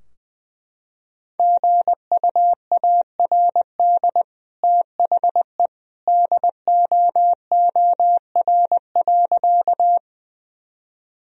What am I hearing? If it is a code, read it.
Morse code, heard as a keyed tone, about 20 wpm, 700 Hz. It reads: GUARD THE DOOR.
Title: Sentences from Top 1000 Words - Review Part 4 - 20wpm